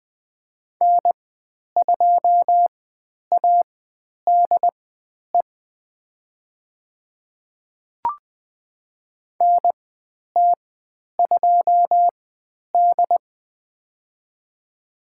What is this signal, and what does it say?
Morse code, heard as a keyed tone, about 20 wpm, spaced as for 10 wpm, 700 Hz. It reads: N2ADE NT2D
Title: 102 Lesson 17g - Letter u - Call Signs 20wpm